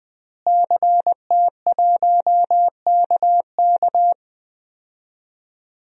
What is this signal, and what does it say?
Morse code, heard as a keyed tone, about 20 wpm, 700 Hz. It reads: CT1KK
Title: DXCC-100 Call Signs - 5 Characters - Repeated 3x - 20wpm